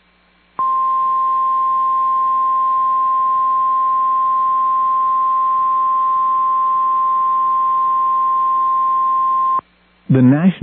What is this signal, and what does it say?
Emergency Alert System (SAME) tones (the 1050 Hz attention tone), with speech. It is heard from 0.5 to 9.6 s.